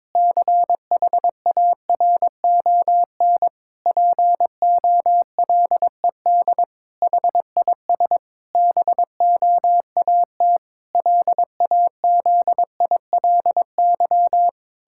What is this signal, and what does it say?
Morse code, heard as a keyed tone, about 22 wpm, 700 Hz. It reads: CHARON POLED HIS BOAT LAZILY
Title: Pangram Challenge v2 - 22wpm